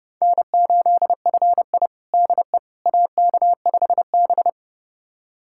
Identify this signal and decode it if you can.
Morse code, heard as a keyed tone, about 30 wpm, 700 Hz. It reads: N8FI DE AK5B
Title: USA Callsign DE Callsign - 4 Characters - 30wpm